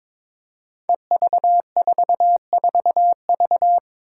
Morse code, heard as a keyed tone, about 22 wpm, 700 Hz. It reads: EV44V
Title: Five Challenging Letter-Number Combinations 22wpm